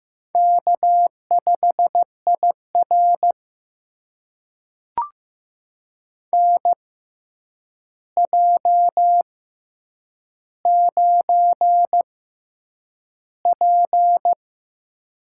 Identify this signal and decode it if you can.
Morse code, heard as a keyed tone, about 15 wpm, spaced as for 8 wpm, 700 Hz. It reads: K5IR NJ9P
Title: USA Call Signs - 6x Character Spacing - 15wpm